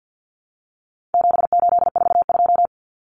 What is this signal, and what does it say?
Morse code, heard as a keyed tone, about 50 wpm, 700 Hz. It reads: 7842
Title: Numbers - 4-Digit 50wpm